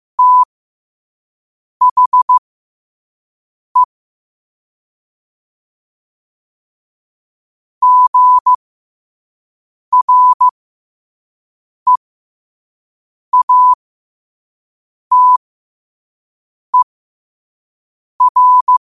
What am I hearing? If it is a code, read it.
Morse code, heard as a keyed tone, about 15 wpm, spaced as for 5 wpm, 1000 Hz. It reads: THE GREATER
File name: qotdfiles / 5wpm